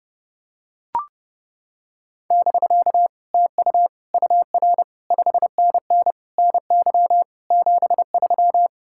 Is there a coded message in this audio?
Yes: Morse code from 2.3 to 8.7 s, heard as a keyed tone, about 30 wpm, 700 Hz. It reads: <BK> TU UR 5NN NY 73